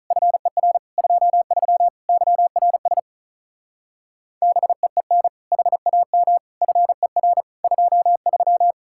Morse code, heard as a keyed tone, about 44 wpm, 700 Hz. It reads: FER 23 YRS BEEN HAM FER 23